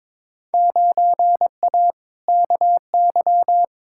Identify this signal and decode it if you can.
Morse code, heard as a keyed tone, about 22 wpm, 700 Hz. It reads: 9A KY
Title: Field Day Exchanges Encore - 22wpm